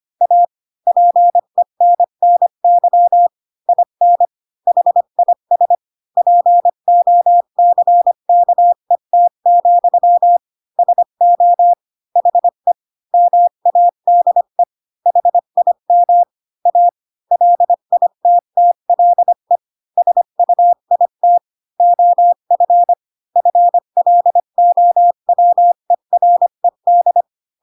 Morse code, heard as a keyed tone, about 25 wpm, 700 Hz. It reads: A PENNY IN HIS POCKET, SO HE MADE HIM A LITTLE SUIT OF FLOWERED
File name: Pinocchio / Pinocchio ch8 20 25 ik1hge